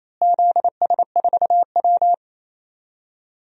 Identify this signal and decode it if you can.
Morse code, heard as a keyed tone, about 27 wpm, 700 Hz. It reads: ZS4W
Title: DXCC-100 Call Signs - 4 Characters - Repeated 2x - 28wpm